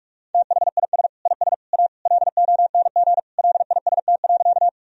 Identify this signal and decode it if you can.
Morse code, heard as a keyed tone, about 46 wpm, 700 Hz. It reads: THIS IS A LONG LIST.
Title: Sentences from Top 1000 Words - Review Part 2 - 45wpm